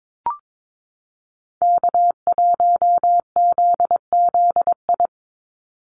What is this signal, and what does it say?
Morse code, heard as a keyed tone, about 22 wpm, 700 Hz. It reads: K1ZZI
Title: Call Signs - 22wpm